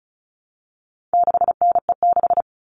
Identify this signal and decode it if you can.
Morse code, heard as a keyed tone, about 34 wpm, 700 Hz. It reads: 6NE6
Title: All Cut Numbers - 4-Digit 35wpm